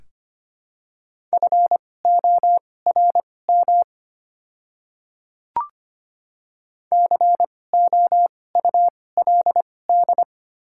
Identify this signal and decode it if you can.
Morse code, heard as a keyed tone, about 25 wpm, spaced as for 18 wpm, 700 Hz. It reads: FORM COULD
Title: 2x Character Spacing - Top 300 Words - 25wpm